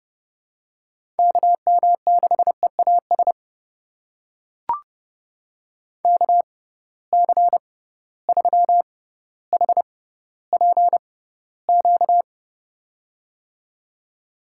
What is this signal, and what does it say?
Morse code, heard as a keyed tone, about 32 wpm, spaced as for 16 wpm, 700 Hz. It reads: KM6EAS KC3HPQ
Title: USA Call Signs - 6 Characters - 6x Character Spacing - 30wpm